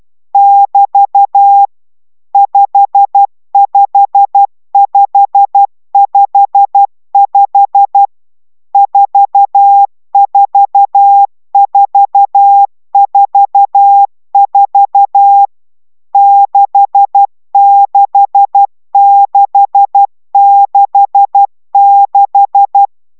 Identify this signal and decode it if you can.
Morse code, heard as a keyed tone, about 12 wpm, 800 Hz. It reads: <BT> 55555 44444 66666
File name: taviro tanulas / cw 69 20 10